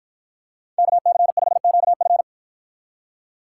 Morse code, heard as a keyed tone, about 53 wpm, 700 Hz. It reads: KC5QF